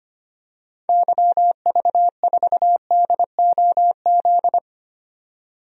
Morse code, heard as a keyed tone, about 25 wpm, 700 Hz. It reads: YV4DOZ
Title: DXCC-100 Call Signs - 6 Characters - 25wpm